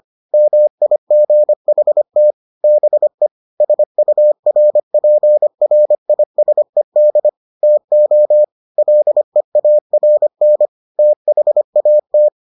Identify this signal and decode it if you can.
Morse code, heard as a keyed tone, about 25 wpm, 600 Hz. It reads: MIGHT BE SURPRISED TO LEARN THAT